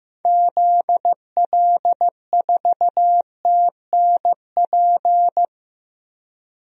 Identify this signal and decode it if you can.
Morse code, heard as a keyed tone, about 15 wpm, 700 Hz. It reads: ZL4TNP
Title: DXCC-100 Call Signs - 6 Characters - Repeated 3x - 15wpm